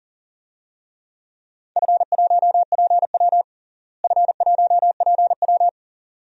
Morse code, heard as a keyed tone, about 42 wpm, 690 Hz. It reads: F1PW F1PW
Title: DXCC-100 Call Signs - 4 Characters - Repeated 2x - 40wpm